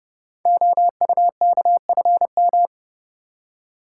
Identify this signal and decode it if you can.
Morse code, heard as a keyed tone, about 30 wpm, 700 Hz. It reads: OUKFM
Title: Code Groups - 5 Letters - 30wpm